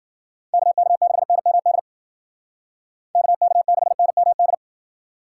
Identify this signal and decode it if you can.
Morse code, heard as a keyed tone, about 51 wpm, 700 Hz. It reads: KK6NMD KK6NMD